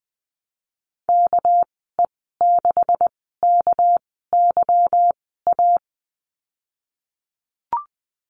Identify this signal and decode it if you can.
Morse code, heard as a keyed tone, about 20 wpm, spaced as for 15 wpm, 700 Hz. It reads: KE6KYA
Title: 200 Lesson 32g - Letter k - Call Signs 20wpm